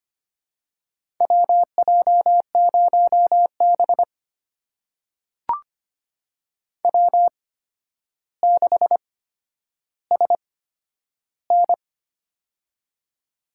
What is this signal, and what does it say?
Morse code, heard as a keyed tone, about 25 wpm, spaced as for 11 wpm, 700 Hz. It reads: WJ0B W6SN
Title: USA Call Signs - 4 Characters - 8x Character Spacing - 25wpm